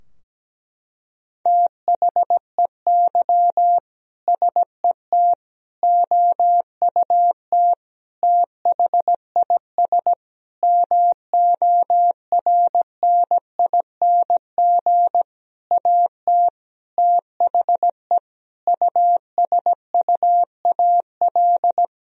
Morse code, heard as a keyed tone, about 17 wpm, 700 Hz. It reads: THEY SET OUT THIS MORNING AT THE USUAL
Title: Sentences from Top 1000 Words - Review Part 2 - 17wpm